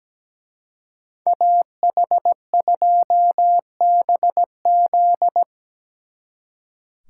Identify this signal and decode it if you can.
Morse code, heard as a keyed tone, about 19 wpm, 700 Hz. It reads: AH2BZ
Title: Speed Racing - USA 2x2 Call Signs 17wpm